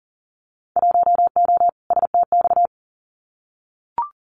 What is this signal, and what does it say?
Morse code, heard as a keyed tone, about 39 wpm, 700 Hz. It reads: 1O STX
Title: Winter Field Day Exchanges 40wpm